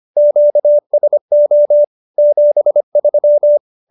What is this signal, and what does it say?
Morse code, heard as a keyed tone, about 25 wpm, 590 Hz. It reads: QSO 73
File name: QSO302 25